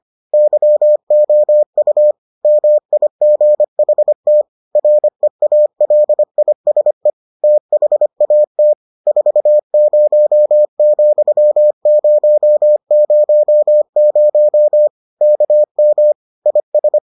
Morse code, heard as a keyed tone, about 25 wpm, 600 Hz. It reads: YOU MIGHT REALISE THAT 40,000 KM IS